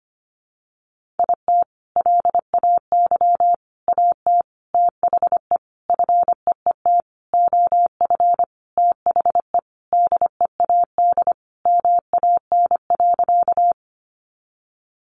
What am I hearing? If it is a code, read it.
Morse code, heard as a keyed tone, about 25 wpm, 700 Hz. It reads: IT LAY AT THE FEET OF THE DEAD MAN.
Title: Sentences from Top 1500 Words 25wpm